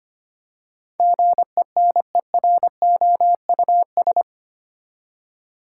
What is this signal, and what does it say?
Morse code, heard as a keyed tone, about 25 wpm, 700 Hz. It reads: GENEROUS
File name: Eight Letter Words - 25wpm